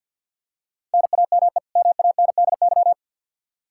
Morse code, heard as a keyed tone, about 52 wpm, 700 Hz. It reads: NAME MANDY
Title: Practice QSO - Exchange 1 - Beginning - 50wpm